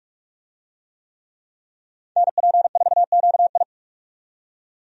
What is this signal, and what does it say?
Morse code, heard as a keyed tone, about 46 wpm, 700 Hz. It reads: NP4QI
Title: USA Call Signs - 5 Characters - 45wpm